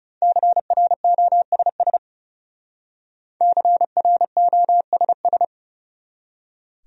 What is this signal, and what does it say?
Morse code, heard as a keyed tone, about 38 wpm, 700 Hz. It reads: CROSS CROSS